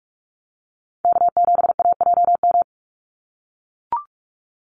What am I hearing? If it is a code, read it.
Morse code, heard as a keyed tone, about 44 wpm, 700 Hz. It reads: K7AJM
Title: Call Signs - 45wpm